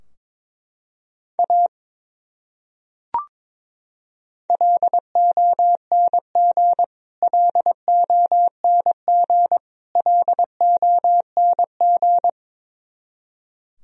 Morse code, heard as a keyed tone, about 21 wpm, 700 Hz. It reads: A LONG LONG LONG
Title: Top 100 Words Repeated 22wpm